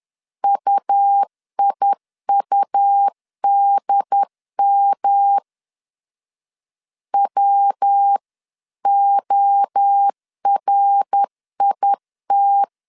Morse code, heard as a keyed tone, about 10 wpm, 800 Hz. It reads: UIUDM WORIT
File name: CW / DET / 1 3 1500C S B 10wpm 3335